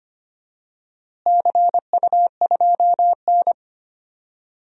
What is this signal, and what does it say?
Morse code, heard as a keyed tone, about 25 wpm, 700 Hz. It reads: CU2N
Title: DXCC-100 Call Signs - 4 Characters - Repeated 2x - 25wpm